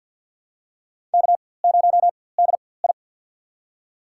Morse code, heard as a keyed tone, about 52 wpm, spaced as for 24 wpm, 700 Hz. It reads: K0DI
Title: USA Call Signs - 4x Character Spacing - 50wpm